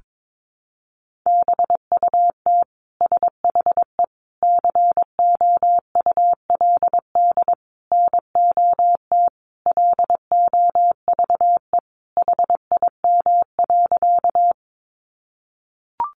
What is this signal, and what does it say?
Morse code, heard as a keyed tone, about 22 wpm, 700 Hz. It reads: BUT SHE COULD NOT LOVE HIM.
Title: Sentences from Top 600 Words 22wpm